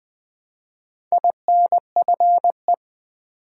Morse code, heard as a keyed tone, about 20 wpm, 700 Hz. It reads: INFE
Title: Build A Word - 1 Letter At A Time - 3 to 5-Letter Words - 20wpm